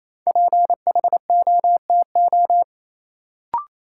Morse code, heard as a keyed tone, about 28 wpm, 700 Hz. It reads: PHOTO
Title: Sentences from Top 1200 Words - New Word Drill - 28wpm